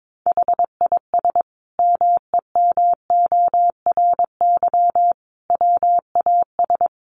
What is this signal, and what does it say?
Morse code, heard as a keyed tone, about 22 wpm, 700 Hz. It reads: HIS MEMORY WAS
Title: Sentences from Top 1300 Words 22wpm